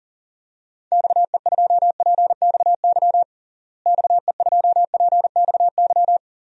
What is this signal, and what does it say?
Morse code, heard as a keyed tone, about 37 wpm, 700 Hz. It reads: XE2PXY XE2PXY
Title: DXCC-100 Call Signs - 6 Characters - Repeated 3x - 40wpm